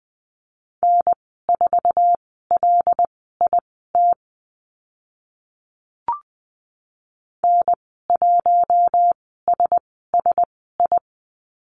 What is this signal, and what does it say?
Morse code, heard as a keyed tone, about 20 wpm, spaced as for 14 wpm, 700 Hz. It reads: N4LIT N1SSI
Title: 076 Lesson 13g - Letter L - Call Signs 20wpm